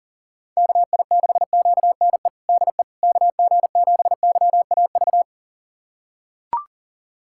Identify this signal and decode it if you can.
Morse code, heard as a keyed tone, about 40 wpm, 700 Hz. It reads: KI6QNE DE KG7YAV
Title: USA Callsign DE Callsign 40wpm